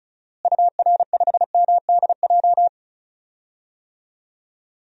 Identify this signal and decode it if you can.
Morse code, heard as a keyed tone, about 36 wpm, 700 Hz. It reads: UR5MDJ
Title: DXCC-100 Call Signs - 6 Characters - Repeated 3x - 35wpm